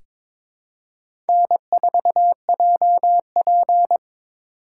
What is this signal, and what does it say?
Morse code, heard as a keyed tone, about 22 wpm, 700 Hz. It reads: N4JP
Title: USA Call Signs - 4 Characters - 3x Character Spacing - 22wpm